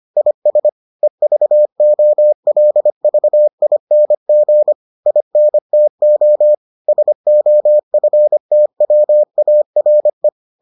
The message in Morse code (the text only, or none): IS EVOLVING INTO SOFTWARE